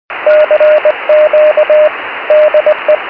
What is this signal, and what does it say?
Morse code, heard as a keyed tone, about 20 wpm, 610 Hz, with radio noise. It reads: CQ DE
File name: 70mhz / sv2dcd pe1itr 70200khz